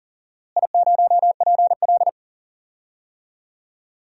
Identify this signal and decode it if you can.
Morse code, heard as a keyed tone, about 40 wpm, 700 Hz. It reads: I0PL
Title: DXCC-100 Call Signs - 4 Characters - Repeated 3x - 40wpm